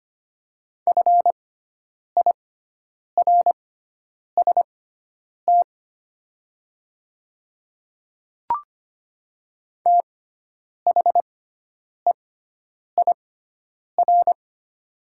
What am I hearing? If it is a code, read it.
Morse code, heard as a keyed tone, about 25 wpm, spaced as for 9 wpm, 700 Hz. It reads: FIRST THEIR
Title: 6x Character Spacing - Top 100 Words - 25wpm